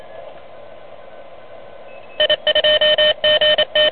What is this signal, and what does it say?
Morse code, heard as a keyed tone, about 28 wpm, 3050 Hz. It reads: I2GT
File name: truSDX